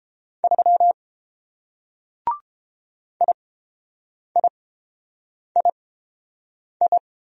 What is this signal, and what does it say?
Morse code, heard as a keyed tone, about 29 wpm, spaced as for 7 wpm, 700 Hz. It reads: 3 IIII